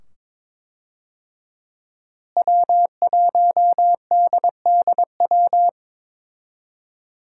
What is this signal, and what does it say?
Morse code, heard as a keyed tone, about 22 wpm, 700 Hz. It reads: W1DDW
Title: USA Call Signs - 5 Characters - Repeated 2x - 22wpm